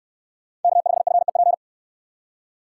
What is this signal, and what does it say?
Morse code, heard as a keyed tone, about 61 wpm, 700 Hz. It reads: K5FEP